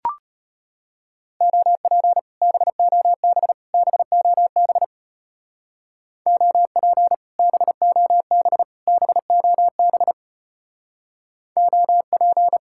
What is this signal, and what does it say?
Morse code, heard as a keyed tone, about 32 wpm, 700 Hz. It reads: OP BOB BOB OP BOB BOB OP